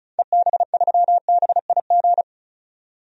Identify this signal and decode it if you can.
Morse code, heard as a keyed tone, about 36 wpm, 700 Hz. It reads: EB3BIG
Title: DXCC-100 Call Signs - 6 Characters - 35wpm